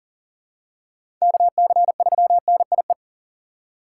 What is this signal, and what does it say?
Morse code, heard as a keyed tone, about 37 wpm, 700 Hz. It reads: KC3NIE
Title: K1USN SST QSO Practice - 40wpm